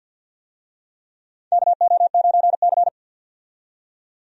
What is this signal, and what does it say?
Morse code, heard as a keyed tone, about 51 wpm, 700 Hz. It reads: KO9C
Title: USA Call Signs - 4 Characters - 50wpm